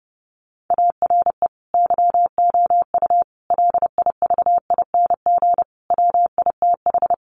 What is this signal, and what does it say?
Morse code, heard as a keyed tone, about 30 wpm, 700 Hz. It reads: ARE YOU LIVING WITH
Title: Sentences from Top 1100 Words 30wpm